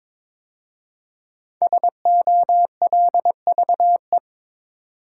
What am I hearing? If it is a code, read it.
Morse code, heard as a keyed tone, about 22 wpm, 700 Hz. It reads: SOLVE
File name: 5 Letter Words - First 3 Letters Given - 22wpm